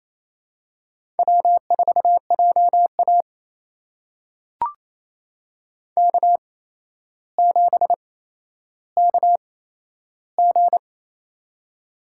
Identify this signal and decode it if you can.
Morse code, heard as a keyed tone, about 27 wpm, spaced as for 12 wpm, 700 Hz. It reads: W4JA K7KG
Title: USA Call Signs - 4 Characters - 8x Character Spacing - 28wpm